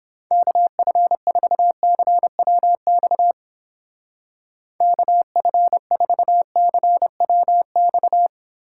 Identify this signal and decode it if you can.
Morse code, heard as a keyed tone, about 29 wpm, 700 Hz. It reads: KF4CWX KF4CWX